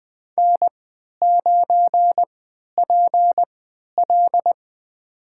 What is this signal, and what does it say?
Morse code, heard as a keyed tone, about 20 wpm, spaced as for 11 wpm, 700 Hz. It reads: N9PL
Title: USA Call Signs - 4 Characters - 3x Character Spacing - 20wpm